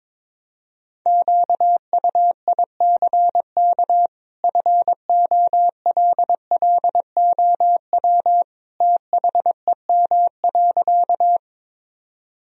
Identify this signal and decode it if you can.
Morse code, heard as a keyed tone, about 22 wpm, 700 Hz. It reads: QUICK FOLLOW THEM.
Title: Sentences from Top 1000 Words - Review Part 1 - 22wpm